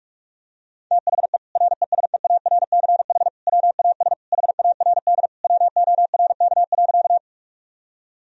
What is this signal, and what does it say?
Morse code, heard as a keyed tone, about 46 wpm, 700 Hz. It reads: THE RESEARCH WAS HARD WORK.
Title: Sentences from Top 1000 Words - Review Part 4 - 45wpm